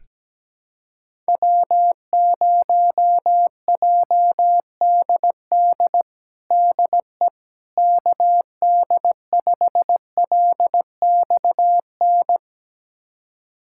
Morse code, heard as a keyed tone, about 17 wpm, 700 Hz. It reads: W0JDD DE KD5LXN
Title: USA Callsign DE Callsign 17wpm